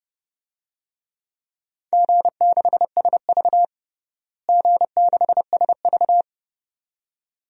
Morse code, heard as a keyed tone, about 30 wpm, 700 Hz. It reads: G6SV G6SV
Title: DXCC-100 Call Signs - 4 Characters - Repeated 2x - 30wpm